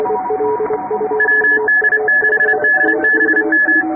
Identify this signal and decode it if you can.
Morse code, heard as a keyed tone, about 30 wpm, 1700 Hz, with noise. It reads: WD6DBM
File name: chatham / 2350-12CW